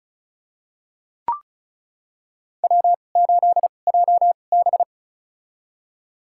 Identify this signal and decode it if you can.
Morse code, heard as a keyed tone, about 36 wpm, spaced as for 25 wpm, 700 Hz. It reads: W8JB
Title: USA Call Signs - 4 Characters - 2x Character Spacing - 35wpm